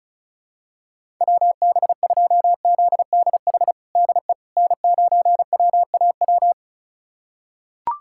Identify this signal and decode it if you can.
Morse code, heard as a keyed tone, about 34 wpm, 700 Hz. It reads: WB2ZDH DE N9WAW
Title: USA Callsign DE Callsign 35wpm